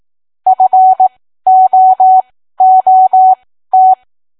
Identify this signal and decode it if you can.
Morse code, heard as a keyed tone, about 18 wpm, spaced as for 13 wpm, 750 Hz. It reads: FOOT